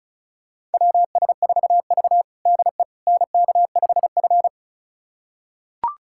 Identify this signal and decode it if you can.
Morse code, heard as a keyed tone, about 36 wpm, 700 Hz. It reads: WS4V DE NK5F
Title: USA Callsign DE Callsign - 4 Characters - 35wpm